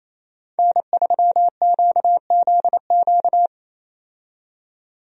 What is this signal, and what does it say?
Morse code, heard as a keyed tone, about 28 wpm, 700 Hz. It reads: N3QZQ
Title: USA Call Signs - 5 Characters - 28wpm